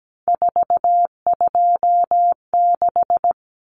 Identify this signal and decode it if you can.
Morse code, heard as a keyed tone, about 17 wpm, 700 Hz. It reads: 426
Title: Common Cut Numbers - 3-Digit 17wpm